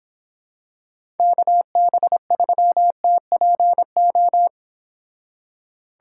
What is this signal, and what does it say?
Morse code, heard as a keyed tone, about 26 wpm, 700 Hz. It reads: KB3TPO